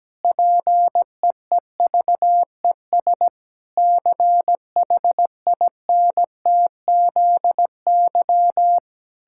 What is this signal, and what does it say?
Morse code, heard as a keyed tone, about 17 wpm, 700 Hz. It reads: PEEVES CHINTZY
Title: Pangram Challenge v2 - 17wpm